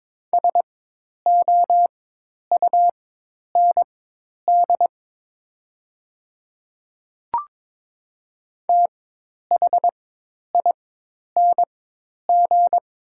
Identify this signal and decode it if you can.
Morse code, heard as a keyed tone, about 22 wpm, spaced as for 10 wpm, 700 Hz. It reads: SOUND THING
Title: 4x Character Spacing - Top 100 Words - 22wpm